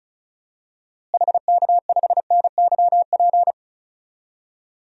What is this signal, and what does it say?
Morse code, heard as a keyed tone, about 36 wpm, 700 Hz. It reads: HK5NYP
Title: DXCC-100 Call Signs - 6 Characters - Repeated 2x - 35wpm